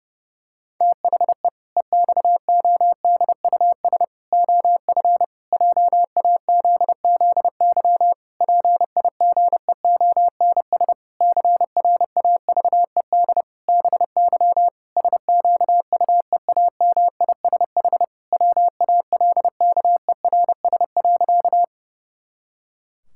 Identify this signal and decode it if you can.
Morse code, heard as a keyed tone, about 30 wpm, 700 Hz. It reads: THE EXODUS OF JAZZY PIGEONS CRAVED BY SQUEAMISH WALKERS.
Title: Pangram Challenge v2 - 30wpm